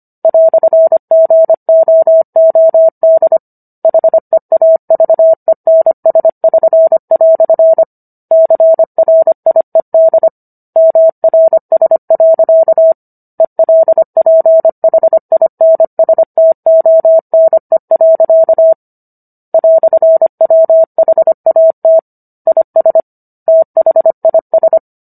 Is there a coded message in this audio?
Yes: Morse code from 0.2 to 24.8 s, heard as a keyed tone, about 25 wpm, 650 Hz. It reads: "GOOD HEAVENS<SN>" CRIED MRS. ELPHINSTONE. "WHAT IS THIS